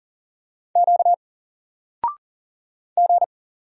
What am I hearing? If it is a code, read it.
Morse code, heard as a keyed tone, about 39 wpm, 700 Hz. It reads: Q G
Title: Single Character (Letter:Number:Punctuation:Prosign) 40wpm